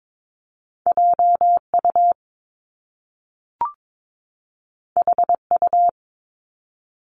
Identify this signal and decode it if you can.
Morse code, heard as a keyed tone, about 22 wpm, 700 Hz. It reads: JU HU
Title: Top Two Letter Combinations in Words 22wpm